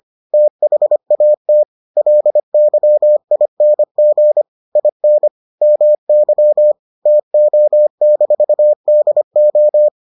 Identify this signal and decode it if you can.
Morse code, heard as a keyed tone, about 25 wpm, 600 Hz. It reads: THAT LYING IN MY TO-DO